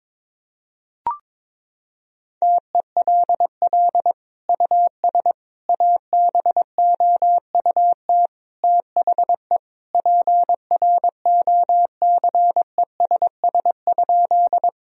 Morse code, heard as a keyed tone, about 22 wpm, 690 Hz. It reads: TELL US ABOUT THE PROCESS?
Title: Sentences from Top 1000 Words - Review Part 3 - 22wpm